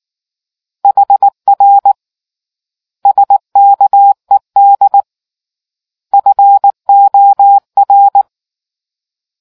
Morse code, heard as a keyed tone, about 19 wpm, 800 Hz. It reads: HR SKED FOR